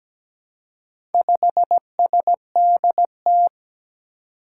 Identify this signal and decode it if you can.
Morse code, heard as a keyed tone, about 17 wpm, 700 Hz. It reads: 5SDT
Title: Four Challenging Letter-Number Combinations 17wpm